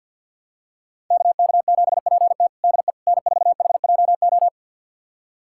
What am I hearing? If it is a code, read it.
Morse code, heard as a keyed tone, about 50 wpm, 700 Hz. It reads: KK7PT DE N4HJO
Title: USA Callsign DE Callsign - 5 Characters - 50wpm